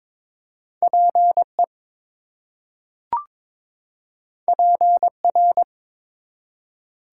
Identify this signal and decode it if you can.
Morse code, heard as a keyed tone, about 22 wpm, 700 Hz. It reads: PE PR
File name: ARRL Sections v2 - 22wpm